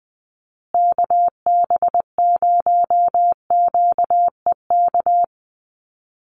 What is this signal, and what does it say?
Morse code, heard as a keyed tone, about 20 wpm, 700 Hz. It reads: KB0QEK
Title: USA Call Signs Repeated 2x 20wpm